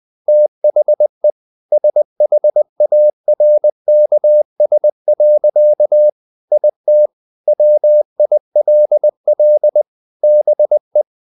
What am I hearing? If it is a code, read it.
Morse code, heard as a keyed tone, about 20 wpm, 600 Hz. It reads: THE SHARKS. IT WILL BE
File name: the old man and the sea part 2 20wpm